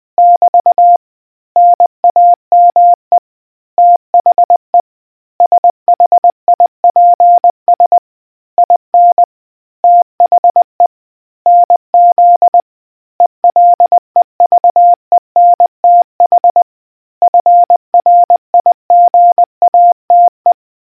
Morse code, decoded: <BT> NAME THE SHIPS IN THE NZ ELEVENTH FRIGATE